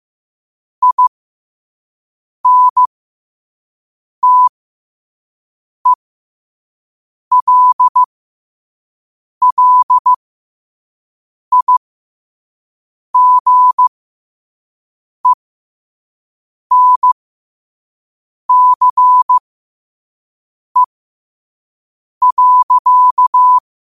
Morse code, heard as a keyed tone, about 15 wpm, spaced as for 5 wpm, 1000 Hz. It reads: INTELLIGENCE.
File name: qotdfiles / 5wpm